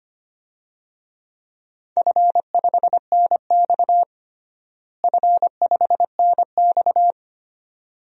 Morse code, heard as a keyed tone, about 25 wpm, 700 Hz. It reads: F5NX F5NX
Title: DXCC-100 Call Signs - 4 Characters - Repeated 2x - 25wpm